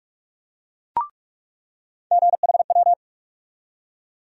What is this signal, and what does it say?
Morse code, heard as a keyed tone, about 47 wpm, 700 Hz. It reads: GHW
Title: USA Call Sign Postfixes - 3 Letters 45wpm